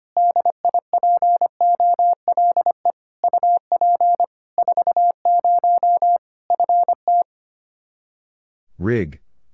Morse code, heard as a keyed tone, about 25 wpm, 700 Hz. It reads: DIPOLE UP 40 FT
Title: K1USN SST QSO Practice 25wpm